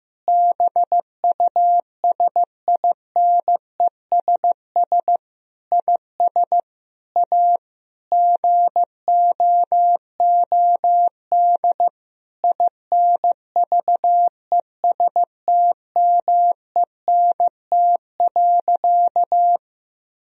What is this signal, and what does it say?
Morse code, heard as a keyed tone, about 15 wpm, 700 Hz. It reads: BUSINESS IS A GOOD INVESTMENT.
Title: Sentences from Top 1000 Words - Review Part 4 - 15wpm